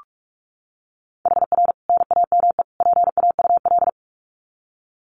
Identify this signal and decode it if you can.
Morse code, heard as a keyed tone, about 44 wpm, 700 Hz. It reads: HR NAME PAUL
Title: Common QSO Phrases 45wpm